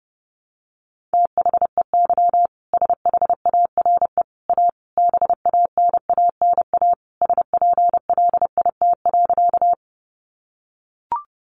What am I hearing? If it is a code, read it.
Morse code, heard as a keyed tone, about 30 wpm, 700 Hz. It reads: THEY SHARE A BANANA SPLIT.
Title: Additional Easy Phrases 30wpm